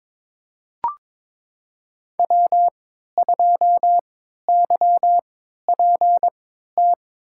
Morse code, heard as a keyed tone, about 22 wpm, spaced as for 12 wpm, 700 Hz. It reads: W2YPT
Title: USA Call Signs - 5 Characters - 3x Character Spacing - 22wpm